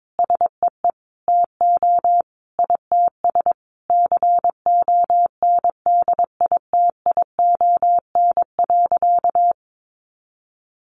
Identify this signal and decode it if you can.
Morse code, heard as a keyed tone, about 22 wpm, 700 Hz. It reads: SEE TO ITS CONDITION.
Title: Sentences from Top 1100 Words 22wpm